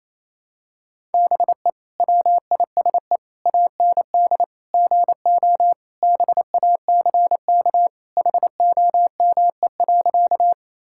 Morse code, heard as a keyed tone, about 28 wpm, 700 Hz. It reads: BE WISE AND GO BACK HOME.
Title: Sentences from Top 1600 Words 28wpm